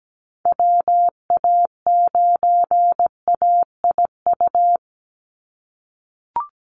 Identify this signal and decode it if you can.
Morse code, heard as a keyed tone, about 17 wpm, 700 Hz. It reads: WA9AIU
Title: USA Call Signs 17wpm